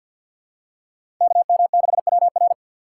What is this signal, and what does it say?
Morse code, heard as a keyed tone, about 47 wpm, 700 Hz. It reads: KM6WR
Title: USA Call Signs - 5 Characters - 3x Character Spacing - 50wpm